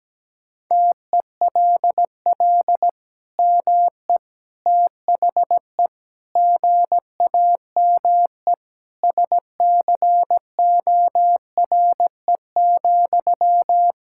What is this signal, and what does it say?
Morse code, heard as a keyed tone, about 17 wpm, 700 Hz. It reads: TELL ME THE GAME SCORE,
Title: Sentences from Top 1600 Words 17wpm